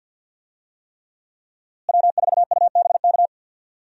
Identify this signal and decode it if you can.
Morse code, heard as a keyed tone, about 47 wpm, 700 Hz. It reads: W4UBK
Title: USA Call Signs - 5 Characters - Repeated 3x - 50wpm